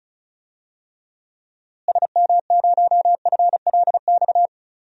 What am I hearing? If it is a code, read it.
Morse code, heard as a keyed tone, about 36 wpm, 700 Hz. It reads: SM0FLX
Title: DXCC-100 Call Signs - 6 Characters - Repeated 3x - 35wpm